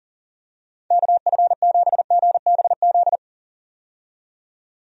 Morse code, heard as a keyed tone, about 40 wpm, 700 Hz. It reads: KF7GBZ
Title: USA Call Signs - 6 Characters - 40wpm